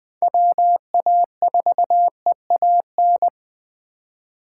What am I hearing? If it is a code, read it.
Morse code, heard as a keyed tone, about 20 wpm, 700 Hz. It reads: WA4EAN
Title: Speed Racing - USA 2x3 Call Signs 17wpm